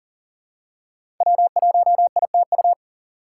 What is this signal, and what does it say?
Morse code, heard as a keyed tone, about 40 wpm, 700 Hz. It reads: W1ITU
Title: USA Call Signs - 5 Characters - 4x Character Spacing - 40wpm